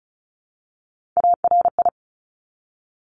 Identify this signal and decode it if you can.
Morse code, heard as a keyed tone, about 36 wpm, 700 Hz. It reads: ARI
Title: Top Three Letter Combinations in Words 35wpm